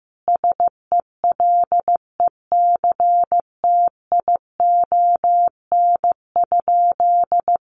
Morse code, heard as a keyed tone, about 15 wpm, 700 Hz. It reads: SELECTION?
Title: Sentences from Top 1400 Words 15wpm